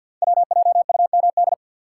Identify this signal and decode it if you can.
Morse code, heard as a keyed tone, about 51 wpm, 700 Hz. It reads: WJUMD
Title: Code Groups - 5 Letters - 50wpm